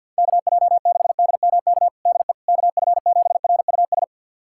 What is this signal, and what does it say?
Morse code, heard as a keyed tone, about 51 wpm, 700 Hz. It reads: KJ6DMK DE KF7RUS
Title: USA Callsign DE Callsign - 6 Characters - 50wpm